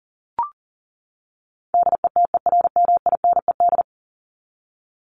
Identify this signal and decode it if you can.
Morse code, heard as a keyed tone, about 40 wpm, 700 Hz. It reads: DETERMINED
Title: ED Suffix - Top 100 Words with Base Words 40wpm